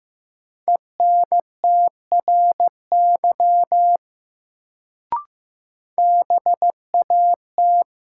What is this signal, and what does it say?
Morse code, heard as a keyed tone, about 15 wpm, 700 Hz. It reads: ENTRY BAT
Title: Sentences from Top 1500 Words - New Word Drill Alt - 15wpm